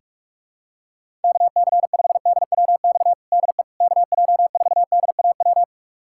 Morse code, heard as a keyed tone, about 46 wpm, 700 Hz. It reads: KC5DWX DE KJ4DAW
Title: USA Callsign DE Callsign - 6 Characters - 45wpm